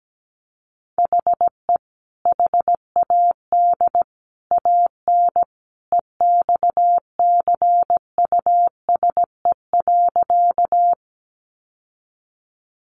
Morse code, heard as a keyed tone, about 17 wpm, 700 Hz. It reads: HE HAD AN EXCUSE.
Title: Sentences from Top 800 Words 17wpm